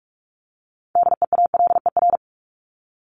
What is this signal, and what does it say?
Morse code, heard as a keyed tone, about 44 wpm, 700 Hz. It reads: DEALER
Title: Top 1001-1250 Words 45wpm